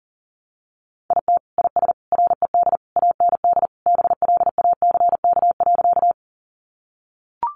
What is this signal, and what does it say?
Morse code, heard as a keyed tone, about 40 wpm, 700 Hz. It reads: IT IS RED AND BLACK.
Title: Additional Easy Phrases 40wpm